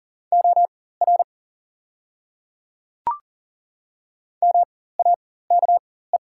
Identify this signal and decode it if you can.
Morse code, heard as a keyed tone, about 40 wpm, spaced as for 19 wpm, 690 Hz. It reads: OR MAKE
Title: 4x Character Spacing - Top 100 Words - 40wpm